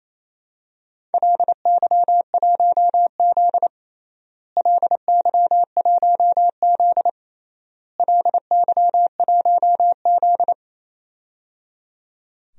Morse code, heard as a keyed tone, about 28 wpm, 700 Hz. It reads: LY1Z LY1Z LY1Z
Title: DXCC-100 Call Signs - 4 Characters - Repeated 3x - 28wpm